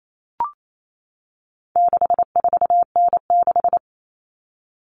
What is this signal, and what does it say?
Morse code, heard as a keyed tone, about 28 wpm, 700 Hz. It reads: 64N6
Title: All Cut Numbers - 4-Digit 28wpm